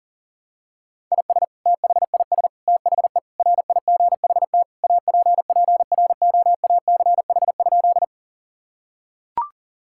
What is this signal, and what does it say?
Morse code, heard as a keyed tone, about 37 wpm, 700 Hz. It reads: IS THIS THE RIGHT APPROACH?
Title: Sentences from Top 1000 Words - Review Part 6 - 40wpm